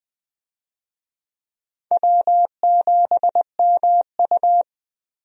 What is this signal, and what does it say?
Morse code, heard as a keyed tone, about 20 wpm, 700 Hz. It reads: W7MU
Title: USA Call Signs - 4 Characters - Repeated 2x - 20wpm